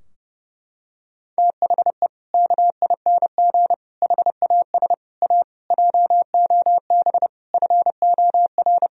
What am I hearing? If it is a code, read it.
Morse code, heard as a keyed tone, about 30 wpm, 700 Hz. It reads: THE KING HAS A JOB FOR
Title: Sentences from Top 1000 Words - Review Part 3 - 30wpm